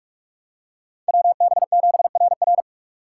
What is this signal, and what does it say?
Morse code, heard as a keyed tone, about 43 wpm, 700 Hz. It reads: WB7RR